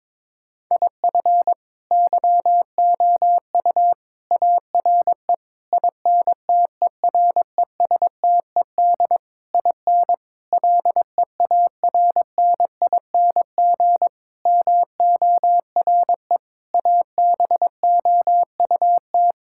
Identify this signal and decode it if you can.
Morse code, heard as a keyed tone, about 22 wpm, 700 Hz. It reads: IF YOU ARE INTERESTED IN LEARNING MORE ABOUT
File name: practice / 22wpm / Sentences with 8-Word Phrases - 22wpm